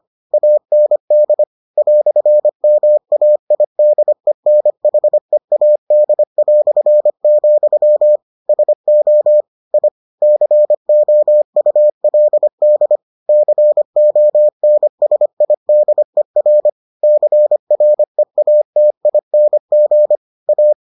Morse code, heard as a keyed tone, about 25 wpm, 600 Hz. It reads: AND "MAIDENHEAD", SO I COULD CONSIDER CREATING A